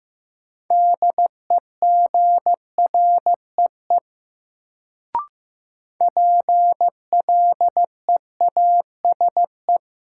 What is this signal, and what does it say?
Morse code, heard as a keyed tone, about 15 wpm, 700 Hz. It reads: DEGREE PLEASE
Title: Sentences from Top 1500 Words - New Word Drill Alt - 15wpm